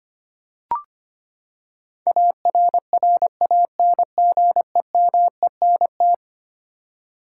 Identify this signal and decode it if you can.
Morse code, heard as a keyed tone, about 25 wpm, 700 Hz. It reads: ARRANGEMENT
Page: morse-code-ninja-practice-25wpm